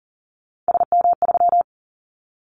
Morse code, heard as a keyed tone, about 38 wpm, 700 Hz. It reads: SM3
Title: Callsign Prefixes for Top 25 DXCC - 40wpm